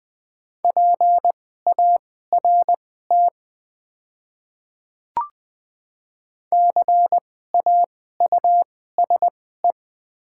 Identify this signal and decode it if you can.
Morse code, heard as a keyed tone, about 20 wpm, spaced as for 15 wpm, 700 Hz. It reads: PART CAUSE
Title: 2x Character Spacing - Top 500 Words - 20wpm